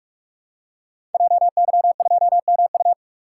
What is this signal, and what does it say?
Morse code, heard as a keyed tone, about 44 wpm, 690 Hz. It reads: JY2MU
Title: Code Groups - 5 Letter:Numbers - 45wpm